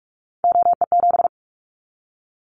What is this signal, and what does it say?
Morse code, heard as a keyed tone, about 46 wpm, 700 Hz. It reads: OE7
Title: Callsign Prefixes for Top 25 DXCC - 45wpm